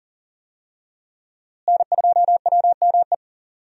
Morse code, heard as a keyed tone, about 40 wpm, 700 Hz. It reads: N2WME
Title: USA Call Signs - 5 Characters - Repeated 2x - 40wpm